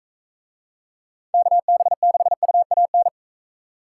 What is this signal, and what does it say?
Morse code, heard as a keyed tone, about 41 wpm, 700 Hz. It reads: KB6UAN